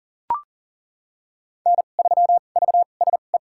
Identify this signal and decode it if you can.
Morse code, heard as a keyed tone, about 37 wpm, spaced as for 28 wpm, 700 Hz. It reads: N3VSE
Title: USA Call Signs - 5 Characters - 2x Character Spacing - 40wpm